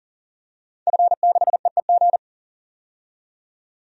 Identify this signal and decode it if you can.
Morse code, heard as a keyed tone, about 40 wpm, 700 Hz. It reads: F6EEG
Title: DXCC-100 Call Signs - 5 Characters - 40wpm